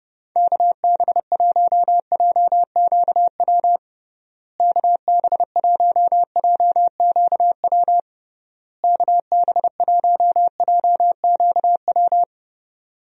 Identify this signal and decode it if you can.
Morse code, heard as a keyed tone, about 30 wpm, 700 Hz. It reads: KB1JQW KB1JQW KB1JQW
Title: USA Call Signs - 6 Characters - Repeated 3x - 30wpm